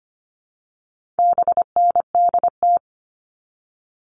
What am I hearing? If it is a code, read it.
Morse code, heard as a keyed tone, about 25 wpm, 700 Hz. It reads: BNDT
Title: All Cut Numbers - 4-Digit 25wpm